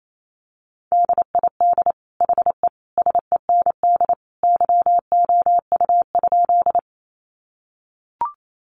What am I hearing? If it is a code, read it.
Morse code, heard as a keyed tone, about 28 wpm, 700 Hz. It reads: DID HE SEND YOU?
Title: Sentences from Top 1500 Words 28wpm